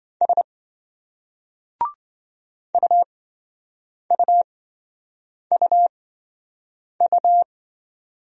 Morse code, heard as a keyed tone, about 28 wpm, spaced as for 7 wpm, 700 Hz. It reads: S UUUU